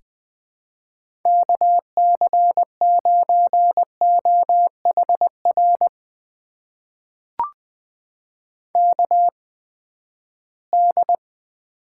Morse code, heard as a keyed tone, about 20 wpm, spaced as for 14 wpm, 700 Hz. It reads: KC9OHR KD
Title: USA Call Signs - 6 Characters - 8x Character Spacing - 20wpm